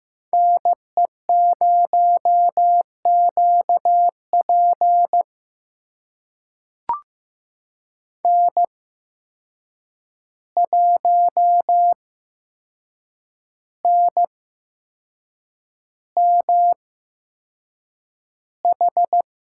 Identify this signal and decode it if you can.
Morse code, heard as a keyed tone, about 15 wpm, spaced as for 6 wpm, 700 Hz. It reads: NE0QP N1NMH
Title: USA Call Signs - 5 Characters - 8x Character Spacing - 15wpm